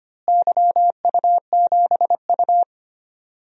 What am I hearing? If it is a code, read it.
Morse code, heard as a keyed tone, about 25 wpm, 700 Hz. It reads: YU7U
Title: DXCC-100 Call Signs - 4 Characters - Repeated 2x - 25wpm